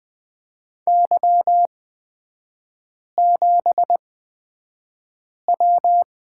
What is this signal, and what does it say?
Morse code, heard as a keyed tone, about 20 wpm, spaced as for 5 wpm, 700 Hz. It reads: Y7W
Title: 255 Lesson 41e - Period - Three Characters 20wpm